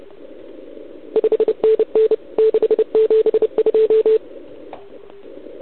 Morse code, heard as a keyed tone, about 31 wpm, 440 Hz. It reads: 5NN 672
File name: ARCHIV / OKDX2019 / OKOM40 KX3